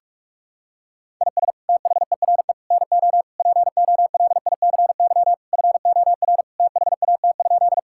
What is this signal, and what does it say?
Morse code, heard as a keyed tone, about 44 wpm, 700 Hz. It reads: IS THERE NO POLICY FOR THAT?
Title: Sentences from Top 1000 Words - Review Part 5 - 45wpm